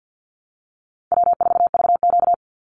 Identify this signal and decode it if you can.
Morse code, heard as a keyed tone, about 50 wpm, 700 Hz. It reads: W4VQ
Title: Easy Call Signs - 50wpm